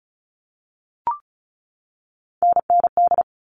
Morse code, heard as a keyed tone, about 34 wpm, 700 Hz. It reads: NND
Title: All Cut Numbers - 3-Digit 35wpm